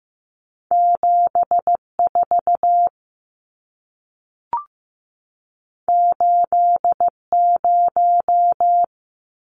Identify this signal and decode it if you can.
Morse code, heard as a keyed tone, about 15 wpm, 700 Hz. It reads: 74 80
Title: Numbers - 2-Digit 15wpm